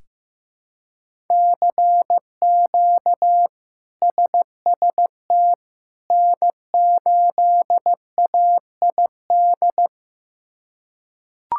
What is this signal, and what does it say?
Morse code, heard as a keyed tone, about 16 wpm, 700 Hz. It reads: CQ SST N8AID
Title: K1USN SST QSO Practice - 15wpm